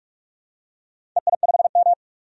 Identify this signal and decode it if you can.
Morse code, heard as a keyed tone, about 46 wpm, 700 Hz. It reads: EI5M
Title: DXCC-100 Call Signs - 4 Characters - 45wpm